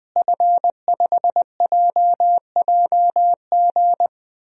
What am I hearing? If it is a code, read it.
Morse code, heard as a keyed tone, about 20 wpm, 700 Hz. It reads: F5JJG
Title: DXCC-100 Call Signs - 5 Characters - Repeated 2x - 20wpm